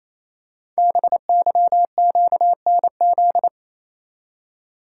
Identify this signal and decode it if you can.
Morse code, heard as a keyed tone, about 28 wpm, 700 Hz. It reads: BYQNZ
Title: Code Groups - 5 Letters - 28wpm